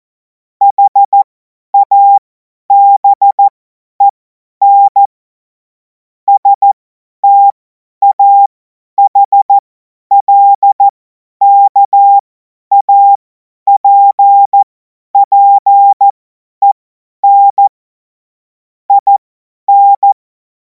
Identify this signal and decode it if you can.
Morse code, heard as a keyed tone, about 14 wpm, spaced as for 10 wpm, 800 Hz. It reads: HABEN STAHLKAPPEN IN
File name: Morsekurs / Witze / Witz 07. April 2026 50 F70